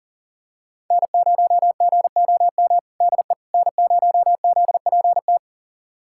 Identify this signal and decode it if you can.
Morse code, heard as a keyed tone, about 40 wpm, 700 Hz. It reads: N0GOM DE N0ZPT
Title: USA Callsign DE Callsign - 5 Characters - 40wpm